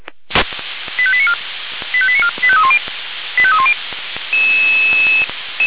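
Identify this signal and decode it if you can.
Five-tone selective call (ZVEI), heard as quick runs of five tones, tones of about 70 ms. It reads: 85894 85894 85310 85310
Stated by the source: ZVEI five-tone selective call 85310 received via SDR.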